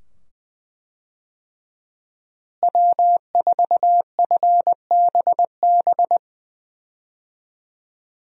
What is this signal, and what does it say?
Morse code, heard as a keyed tone, about 21 wpm, 700 Hz. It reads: W4FBB
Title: USA Call Signs - 5 Characters - Repeated 2x - 20wpm